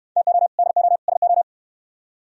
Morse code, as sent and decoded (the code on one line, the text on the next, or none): - --- / -.. --- / ... ---
TO DO SO